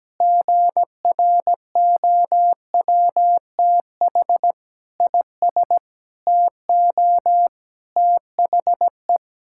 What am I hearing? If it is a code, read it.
Morse code, heard as a keyed tone, about 17 wpm, 700 Hz. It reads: GROWTH IS TO THE
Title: Sentences from Top 1000 Words - Review Part 5 - 17wpm